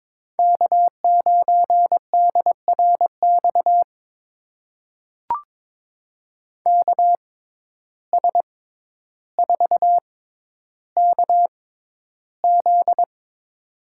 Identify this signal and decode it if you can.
Morse code, heard as a keyed tone, about 21 wpm, spaced as for 11 wpm, 700 Hz. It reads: K9DRX KS4KZ
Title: USA Call Signs - 6x Character Spacing - 22wpm